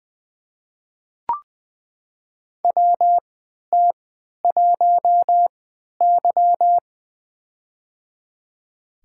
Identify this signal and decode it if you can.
Morse code, heard as a keyed tone, about 20 wpm, spaced as for 11 wpm, 700 Hz. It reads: WT1Y
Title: USA Call Signs - 3x Character Spacing - 20wpm